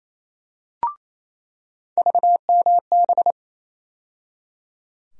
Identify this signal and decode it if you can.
Morse code, heard as a keyed tone, about 28 wpm, 700 Hz. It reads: VMB
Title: USA Call Sign Postfixes - 3 Letters 28wpm